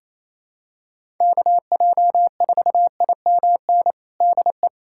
Morse code, heard as a keyed tone, about 28 wpm, 700 Hz. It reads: KJ4IMN DE
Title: USA Callsign DE Callsign - 6 Characters - 28wpm